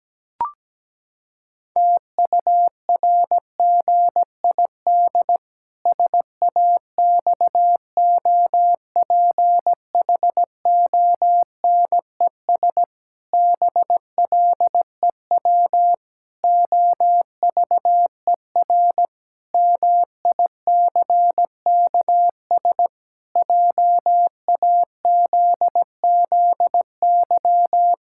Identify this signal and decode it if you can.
Morse code, heard as a keyed tone, about 17 wpm, 700 Hz. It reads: TURGID SAXOPHONES BLEW OVER MICKS JAZZY
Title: Pangram Challenge v2 - 17wpm